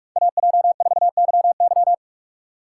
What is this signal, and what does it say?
Morse code, heard as a keyed tone, about 46 wpm, 700 Hz. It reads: AJ4YY
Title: USA Call Signs - 5 Characters - 4x Character Spacing - 45wpm